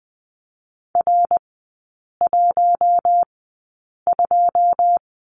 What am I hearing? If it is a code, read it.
Morse code, heard as a keyed tone, about 20 wpm, spaced as for 8 wpm, 690 Hz. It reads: R12
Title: 093 Lesson 16e - Letter c - Three Characters 20wpm